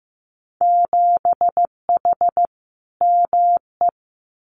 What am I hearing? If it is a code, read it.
Morse code, heard as a keyed tone, about 15 wpm, 700 Hz. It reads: 7H ME
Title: Winter Field Day Exchanges Encore 15wpm